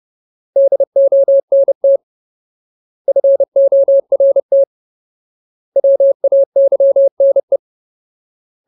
Morse code, heard as a keyed tone, about 30 wpm, 550 Hz. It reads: DONT FORT WAYNE
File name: Ultimate Word List / 30WPM Ultimate